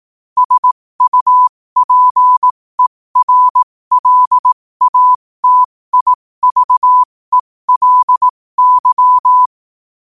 Morse code, heard as a keyed tone, about 18 wpm, spaced as for 15 wpm, 1000 Hz. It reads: SUPERLATIVELY
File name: qotdfiles / 15wpm